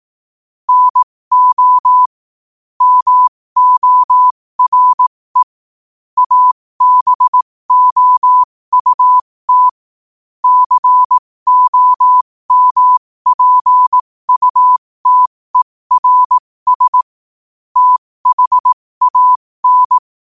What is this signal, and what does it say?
Morse code, heard as a keyed tone, about 18 wpm, spaced as for 15 wpm, 1000 Hz. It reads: NO MORE ABOUT COMPUTERS THAN
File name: qotdfiles / 15wpm